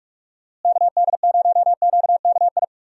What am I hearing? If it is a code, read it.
Morse code, heard as a keyed tone, about 46 wpm, 700 Hz. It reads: KD0QKI
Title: USA Call Signs - 6 Characters - Repeated 2x - 45wpm